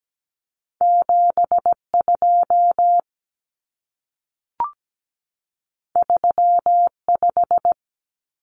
Morse code, heard as a keyed tone, about 17 wpm, 700 Hz. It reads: 72 35
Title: Numbers - 2-Digit 17wpm